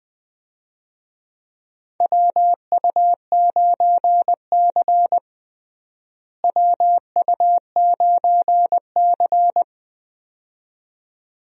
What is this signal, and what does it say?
Morse code, heard as a keyed tone, about 21 wpm, 700 Hz. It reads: WU9C WU9C
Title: USA Call Signs - 4 Characters - Repeated 2x - 20wpm